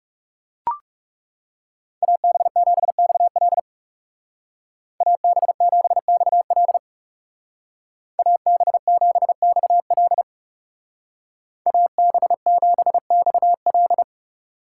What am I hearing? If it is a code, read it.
Morse code, heard as a keyed tone, about 35 wpm, 700 Hz. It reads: AB7XL AB7XL AB7XL AB7XL
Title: Speed Racing - USA 2x2 Call Signs 30wpm